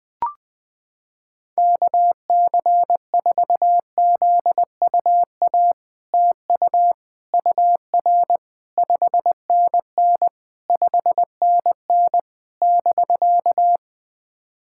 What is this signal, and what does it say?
Morse code, heard as a keyed tone, about 20 wpm, 700 Hz. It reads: KC4ZUA TU UR 5NN 5NN <BK>
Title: POTA QSO Practice - 20wpm